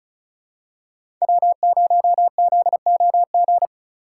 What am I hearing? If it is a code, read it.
Morse code, heard as a keyed tone, about 36 wpm, 700 Hz. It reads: W0ZOG
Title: USA Call Signs - 5 Characters - 3x Character Spacing - 35wpm